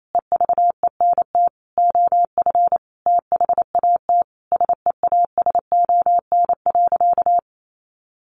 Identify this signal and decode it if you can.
Morse code, heard as a keyed tone, about 28 wpm, 700 Hz. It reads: EVENT OF THAT SEASON.
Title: Sentences from Top 1400 Words 28wpm